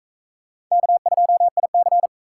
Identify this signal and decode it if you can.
Morse code, heard as a keyed tone, about 38 wpm, 700 Hz. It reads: K2IC